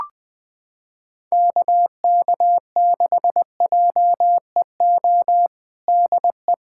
Morse code, heard as a keyed tone, about 20 wpm, 700 Hz. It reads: KK6JEO DE
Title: USA Callsign DE Callsign 20wpm